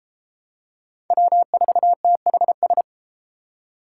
Morse code, heard as a keyed tone, about 32 wpm, 700 Hz. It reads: W4THS